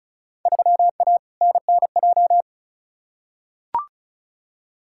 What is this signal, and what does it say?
Morse code, heard as a keyed tone, about 36 wpm, 700 Hz. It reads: 3A NNJ
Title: Field Day Exchanges - 35wpm